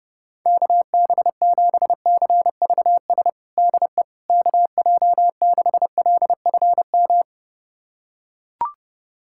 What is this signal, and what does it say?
Morse code, heard as a keyed tone, about 32 wpm, 700 Hz. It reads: KB7CVS DE KJ6LFM
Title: USA Callsign DE Callsign - 6 Characters - 30wpm